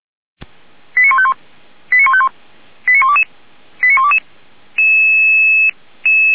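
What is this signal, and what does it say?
Five-tone selective call (ZVEI), heard as quick runs of five tones, tones of about 70 ms. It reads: 89151 89151 89122 89122
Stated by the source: ZVEI five-tone selective call received on SDR.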